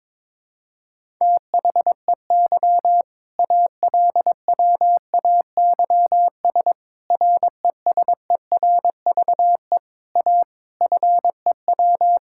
Morse code, heard as a keyed tone, about 22 wpm, 700 Hz. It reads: THEY ALWAYS RESERVE A FEW
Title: Sentences from Top 1000 Words - Review Part 6 - 22wpm